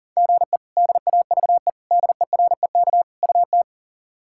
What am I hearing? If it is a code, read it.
Morse code, heard as a keyed tone, about 40 wpm, 700 Hz. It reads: GE DAVE DEREK UT
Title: K1USN SST QSO Practice - 40wpm